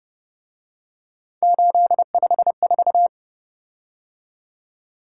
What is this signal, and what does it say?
Morse code, heard as a keyed tone, about 30 wpm, 700 Hz. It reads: 854
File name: Speed-Racing - Three Letter-Number Combinations - 25wpm